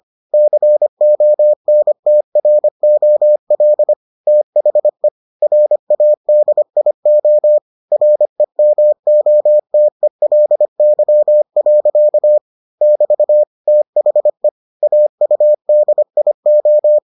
Morse code, heard as a keyed tone, about 25 wpm, 600 Hz. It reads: CONTROL THE RADIO REMOTELY. <BT> THE AUDIO